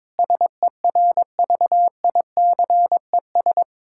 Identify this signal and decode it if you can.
Morse code, heard as a keyed tone, about 22 wpm, 700 Hz. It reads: SERVICES
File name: practice / 22wpm / Build A Phrase - 5-Word Phrases - 22wpm